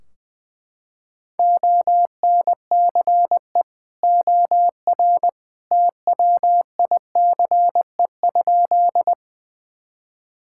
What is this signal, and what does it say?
Morse code, heard as a keyed tone, about 19 wpm, 700 Hz. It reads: ONCE OR TWICE?
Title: Sentences from Top 1600 Words 20wpm